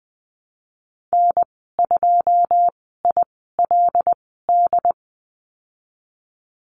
Morse code, heard as a keyed tone, about 20 wpm, spaced as for 15 wpm, 700 Hz. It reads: N2ILD
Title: 082 Lesson 14g - Number 2 - Call Signs 20wpm